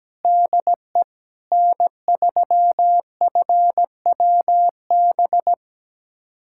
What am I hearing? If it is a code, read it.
Morse code, heard as a keyed tone, about 17 wpm, 700 Hz. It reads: DE N3FWB
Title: USA Callsign DE Callsign - 5 Characters - 17wpm